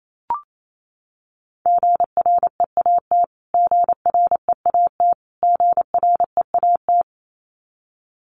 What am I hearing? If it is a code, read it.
Morse code, heard as a keyed tone, about 28 wpm, 700 Hz. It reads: GREAT GREAT GREAT
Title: Top 200 Words Repeated 28wpm